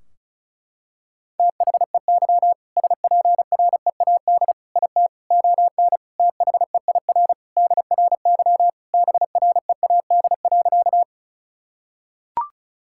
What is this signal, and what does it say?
Morse code, heard as a keyed tone, about 34 wpm, 700 Hz. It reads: THEY SPREAD IT ON THEIR DRY BREAD.
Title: Sentences from Top 1600 Words 35wpm